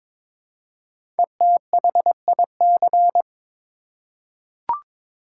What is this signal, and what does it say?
Morse code, heard as a keyed tone, about 22 wpm, 700 Hz. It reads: ETHIC
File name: 5 Letter Words - First 3 Letters Given - 22wpm